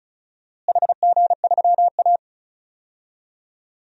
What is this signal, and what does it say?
Morse code, heard as a keyed tone, about 34 wpm, 700 Hz. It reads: HG3A
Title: DXCC-100 Call Signs - 4 Characters - Repeated 2x - 35wpm